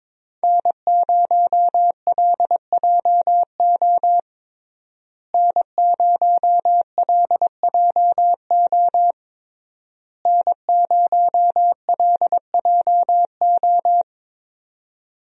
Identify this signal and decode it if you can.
Morse code, heard as a keyed tone, about 21 wpm, 700 Hz. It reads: N0LJO N0LJO N0LJO
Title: USA Call Signs - 5 Characters - Repeated 3x - 22wpm